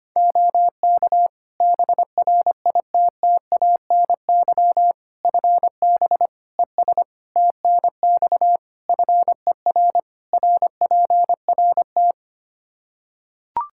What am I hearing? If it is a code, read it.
Morse code, heard as a keyed tone, about 25 wpm, 700 Hz. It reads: OK BRITTANY FB ES TNX FER RPRT
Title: Practice QSO - Exchange 2 - Info - 25wpm